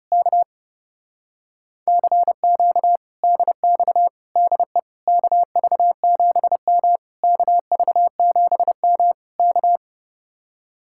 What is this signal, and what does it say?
Morse code, heard as a keyed tone, about 28 wpm, 700 Hz. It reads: K CQ DX DE KV7M KV7M K
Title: Speed-Racing Common QSO Phrases - 30wpm